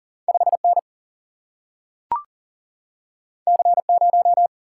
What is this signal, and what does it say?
Morse code, heard as a keyed tone, about 40 wpm, 700 Hz. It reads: 5N C0
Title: Single-Letter Single-Number 40wpm